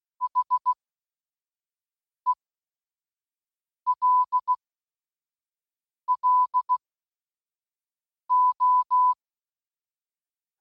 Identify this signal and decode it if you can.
Morse code, heard as a keyed tone, about 16 wpm, 1000 Hz. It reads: H E L L O